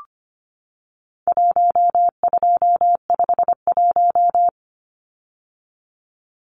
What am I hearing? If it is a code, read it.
Morse code, heard as a keyed tone, about 25 wpm, 700 Hz. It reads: 1251
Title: Numbers - Contest Style - 4-Digit 25wpm